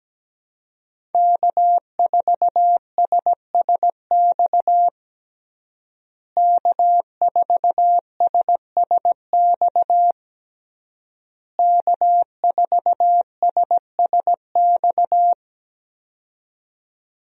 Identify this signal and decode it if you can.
Morse code, heard as a keyed tone, about 17 wpm, 700 Hz. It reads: K4SSX K4SSX K4SSX
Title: USA Call Signs - 5 Characters - Repeated 3x - 17wpm